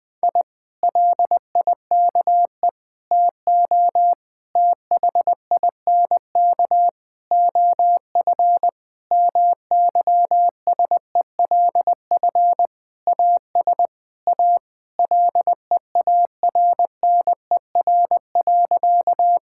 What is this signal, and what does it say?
Morse code, heard as a keyed tone, about 20 wpm, 700 Hz. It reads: I LIKE TO THINK OF MYSELF AS A LEARNER.
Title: Sentences with 8-Word Phrases - Spoken Beginning - 20wpm